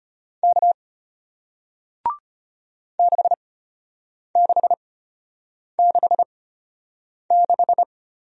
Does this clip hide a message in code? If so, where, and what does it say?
Yes: Morse code from 0.4 to 7.8 s, heard as a keyed tone, about 33 wpm, spaced as for 8 wpm, 700 Hz. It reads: K 6666